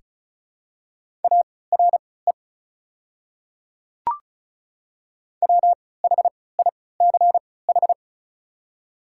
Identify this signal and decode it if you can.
Morse code, heard as a keyed tone, about 36 wpm, spaced as for 20 wpm, 690 Hz. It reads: ARE WHICH
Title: 3x Character Spacing - Top 300 Words - 35wpm